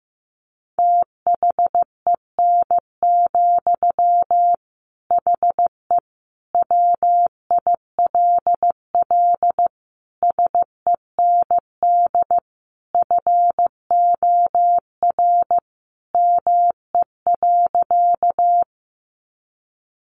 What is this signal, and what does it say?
Morse code, heard as a keyed tone, about 15 wpm, 700 Hz. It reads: THEN, HE WILL SEND FOR ME.
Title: Sentences from Top 1500 Words 15wpm